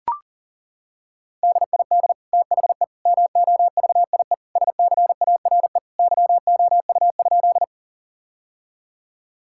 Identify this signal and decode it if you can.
Morse code, heard as a keyed tone, about 40 wpm, 700 Hz. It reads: DID THE MOVIE SCARE YOU?
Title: Sentences from Top 1000 Words - Review Part 5 - 40wpm